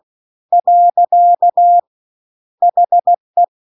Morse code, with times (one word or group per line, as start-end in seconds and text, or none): 0.5-1.8 .
2.6-3.4 HE